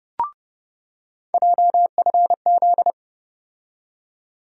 Morse code, heard as a keyed tone, about 29 wpm, 700 Hz. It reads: JFZ
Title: USA Call Sign Postfixes - 3 Letters 30wpm